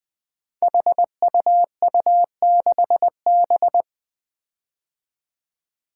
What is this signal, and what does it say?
Morse code, heard as a keyed tone, about 20 wpm, 700 Hz. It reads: HUU6B
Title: Code Groups - 5 Letter:Numbers - 20wpm